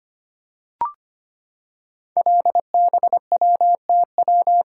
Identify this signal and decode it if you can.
Morse code, heard as a keyed tone, about 25 wpm, 700 Hz. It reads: LBWTW
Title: Code Groups - 5 Letter:Numbers - 25wpm